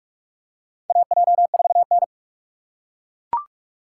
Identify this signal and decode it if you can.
Morse code, heard as a keyed tone, about 41 wpm, 690 Hz. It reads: AJ4N
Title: USA Call Signs - 4 Characters - 4x Character Spacing - 45wpm